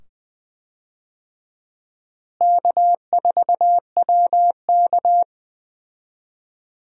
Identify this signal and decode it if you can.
Morse code, heard as a keyed tone, about 20 wpm, 700 Hz. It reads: K4WK
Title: USA Call Signs - 4 Characters - Repeated 2x - 20wpm